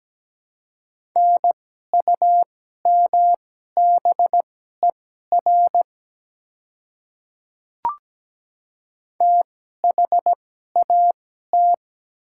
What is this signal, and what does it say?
Morse code, heard as a keyed tone, about 17 wpm, spaced as for 12 wpm, 690 Hz. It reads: NUMBER THAT
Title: 2x Character Spacing - Top 100 Words - 17wpm